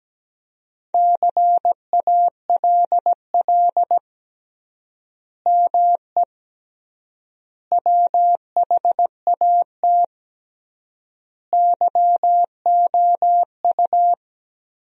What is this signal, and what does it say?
Morse code, heard as a keyed tone, about 17 wpm, 700 Hz. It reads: CALL ME WHAT YOU
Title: 3x Word-Spacing - Sentences from Top 200 Words - 17wpm